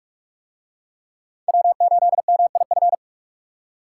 Morse code, heard as a keyed tone, about 41 wpm, 700 Hz. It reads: W8MIF